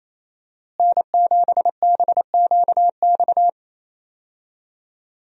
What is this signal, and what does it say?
Morse code, heard as a keyed tone, about 28 wpm, 700 Hz. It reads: N7BQX
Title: USA Call Signs - 5 Characters - Repeated 3x - 28wpm